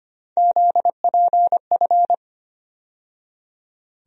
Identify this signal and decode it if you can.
Morse code, heard as a keyed tone, about 25 wpm, 700 Hz. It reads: ZPF
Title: USA Call Sign Postfixes - 3 Letters 25wpm